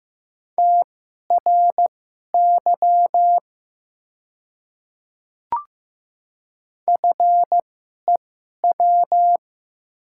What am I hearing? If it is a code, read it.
Morse code, heard as a keyed tone, about 15 wpm, spaced as for 11 wpm, 700 Hz. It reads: TRY FEW
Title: 2x Character Spacing - Top 500 Words - 15wpm